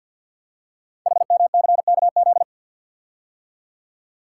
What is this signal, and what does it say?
Morse code, heard as a keyed tone, about 48 wpm, 700 Hz. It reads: HMCKZ
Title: Code Groups - 5 Letters - 50wpm